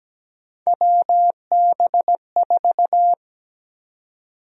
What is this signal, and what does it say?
Morse code, heard as a keyed tone, about 17 wpm, 700 Hz. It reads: WB4
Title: USA Call Sign Prefixes - Repeated 2x - 17wpm